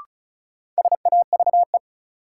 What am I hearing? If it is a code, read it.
Morse code, heard as a keyed tone, about 34 wpm, 700 Hz. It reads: SAVE
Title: Sentences from Top 1100 Words - New Word Drill - 35wpm